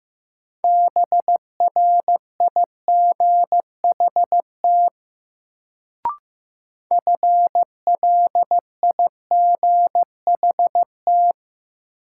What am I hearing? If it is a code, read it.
Morse code, heard as a keyed tone, about 15 wpm, 700 Hz. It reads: BRIGHT FLIGHT
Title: Sentences from Top 1000 Words - New Word Drill - 15wpm